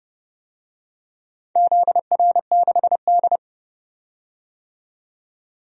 Morse code, heard as a keyed tone, about 30 wpm, 700 Hz. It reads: ZR6D
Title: DXCC-100 Call Signs - 4 Characters - 30wpm